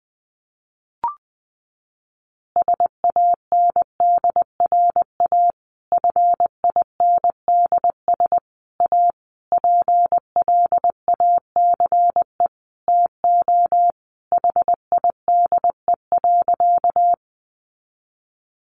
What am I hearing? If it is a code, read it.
Morse code, heard as a keyed tone, about 20 wpm, 700 Hz. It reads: SANDRA FINDS A PLACE TO HIDE.
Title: Easy Phrases 20wpm